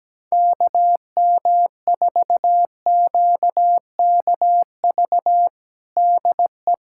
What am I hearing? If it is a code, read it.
Morse code, heard as a keyed tone, about 17 wpm, 700 Hz. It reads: KM4QKV DE
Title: USA Callsign DE Callsign - 6 Characters - 17wpm